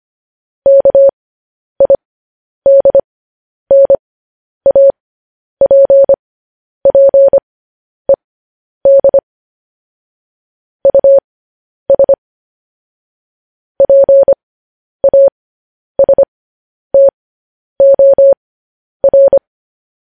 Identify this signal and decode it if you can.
Morse code, heard as a keyed tone, about 25 wpm, spaced as for 10 wpm, 550 Hz. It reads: KIDNAPPED US PASTOR
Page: news-headlines-in-morse-code-at-10-wpm